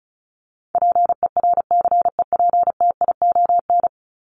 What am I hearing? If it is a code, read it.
Morse code, heard as a keyed tone, about 34 wpm, 700 Hz. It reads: PERCEPTION
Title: Top 1751-2000 Words 35wpm